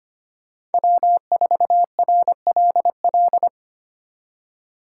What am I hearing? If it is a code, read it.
Morse code, heard as a keyed tone, about 31 wpm, 700 Hz. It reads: W4RLL
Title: Speed Racing - USA 1x3 Call Signs 25wpm